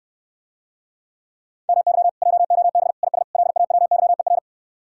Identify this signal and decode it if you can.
Morse code, heard as a keyed tone, about 69 wpm, 700 Hz. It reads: MY JOB IS LAWYER